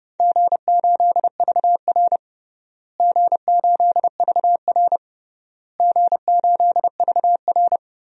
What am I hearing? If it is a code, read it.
Morse code, heard as a keyed tone, about 32 wpm, 700 Hz. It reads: G8VR G8VR G8VR
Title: DXCC-100 Call Signs - 4 Characters - Repeated 3x - 30wpm